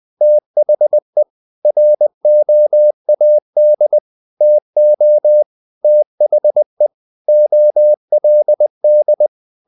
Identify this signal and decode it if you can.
Morse code, heard as a keyed tone, about 20 wpm, 600 Hz. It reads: THE ROAD TO THE OLD
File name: the old man and the sea / the old man and the sea part 1 20wpm